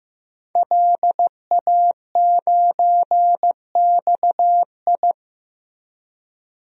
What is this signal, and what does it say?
Morse code, heard as a keyed tone, about 15 wpm, 700 Hz. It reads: LA9XI
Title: DXCC-100 Call Signs - 5 Characters - Repeated 3x - 15wpm